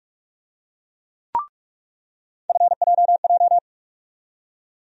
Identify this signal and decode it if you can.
Morse code, heard as a keyed tone, about 45 wpm, 700 Hz. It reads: FJJ
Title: USA Call Sign Postfixes - 3 Letters 45wpm